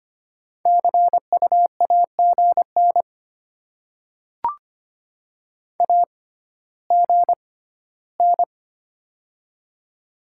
Morse code, heard as a keyed tone, about 25 wpm, spaced as for 14 wpm, 700 Hz. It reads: CU AGN AGN
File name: Top 20 QSO Elements - 6x Character Spacing - 25wpm